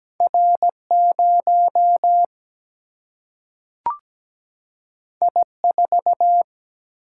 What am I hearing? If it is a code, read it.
Morse code, heard as a keyed tone, about 17 wpm, 700 Hz. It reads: R0 I4
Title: Single-Letter Single-Number 17wpm